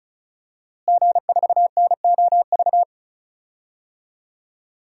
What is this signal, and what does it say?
Morse code, heard as a keyed tone, about 36 wpm, 700 Hz. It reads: G4NOV
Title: DXCC-100 Call Signs - 5 Characters - Repeated 2x - 35wpm